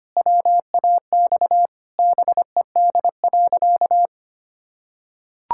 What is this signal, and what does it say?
Morse code, heard as a keyed tone, about 25 wpm, 700 Hz. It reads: WAX BED.
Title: Pangram Challenge v2 - 25wpm